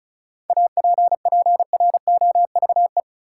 Morse code, heard as a keyed tone, about 36 wpm, 700 Hz. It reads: APPROVE
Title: Seven Letter Words - 35wpm